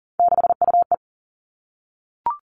Morse code, heard as a keyed tone, about 40 wpm, 700 Hz. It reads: 6UE
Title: Three Challenging Letter-Number Combinations 40wpm